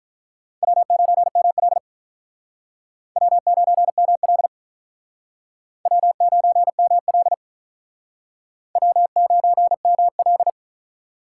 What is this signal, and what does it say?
Morse code, heard as a keyed tone, about 48 wpm, 700 Hz. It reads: W9ML W9ML W9ML W9ML